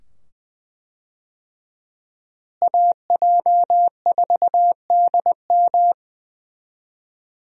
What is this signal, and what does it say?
Morse code, heard as a keyed tone, about 20 wpm, 700 Hz. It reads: AJ4DM
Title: USA Call Signs - 5 Characters - Repeated 2x - 20wpm